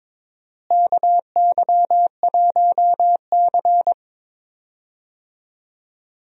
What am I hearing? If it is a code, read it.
Morse code, heard as a keyed tone, about 22 wpm, 700 Hz. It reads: KY1C
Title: USA Call Signs - 4 Characters - Repeated 2x - 22wpm